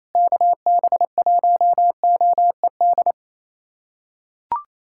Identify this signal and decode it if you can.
Morse code, heard as a keyed tone, about 28 wpm, 700 Hz. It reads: KB1OED
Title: USA Callsign DE Callsign - 6 Characters - 28wpm